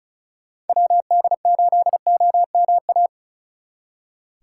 Morse code, heard as a keyed tone, about 43 wpm, 700 Hz. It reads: WD8OMA
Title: Speed Racing - USA Call Signs 35wpm